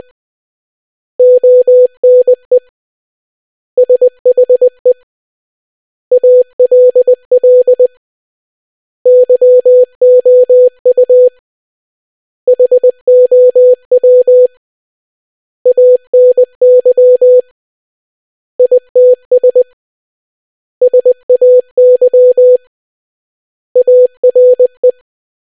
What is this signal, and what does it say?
Morse code, heard as a keyed tone, about 20 wpm, 500 Hz. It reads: ONE SHE ALL YOU HOW ANY ITS SAY ARE